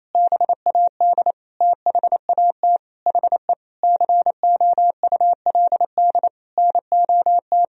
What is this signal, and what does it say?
Morse code, heard as a keyed tone, about 28 wpm, 700 Hz. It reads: BAD THAT HE COULD NOT
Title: Sentences from Top 1000 Words - Review Part 4 - 28wpm